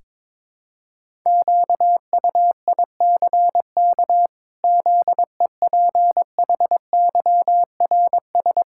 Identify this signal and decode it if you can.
Morse code, heard as a keyed tone, about 22 wpm, 700 Hz. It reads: QUICK ZEPHYRS
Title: Pangram Challenge v2 - 22wpm